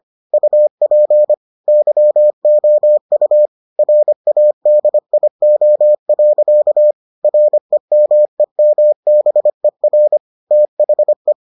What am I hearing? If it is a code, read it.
Morse code, heard as a keyed tone, about 25 wpm, 600 Hz. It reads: UP YOU RADIO. REMEMBER THE